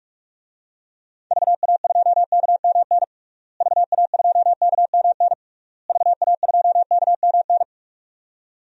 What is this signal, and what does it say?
Morse code, heard as a keyed tone, about 46 wpm, 700 Hz. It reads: VA2KMN VA2KMN VA2KMN
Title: DXCC-100 Call Signs - 6 Characters - Repeated 3x - 45wpm